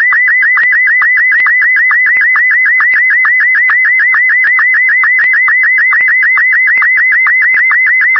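SSTV (slow-scan television), Martin 1.
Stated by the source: SSTV image sent as sound, frequency-modulated tones for pixel brightness and colour, as in images sent from the ISS.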